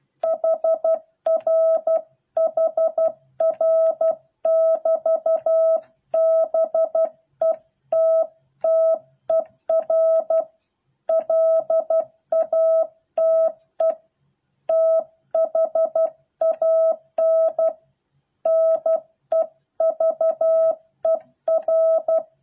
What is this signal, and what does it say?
Morse code, heard as a keyed tone, about 12 wpm, 650 Hz. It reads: HR HR <BT> BETTER LATE THAN NEVER